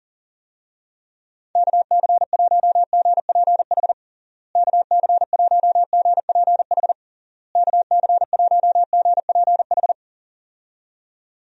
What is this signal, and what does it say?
Morse code, heard as a keyed tone, about 40 wpm, 700 Hz. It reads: KC1GPH KC1GPH KC1GPH
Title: USA Call Signs - 6 Characters - Repeated 3x - 40wpm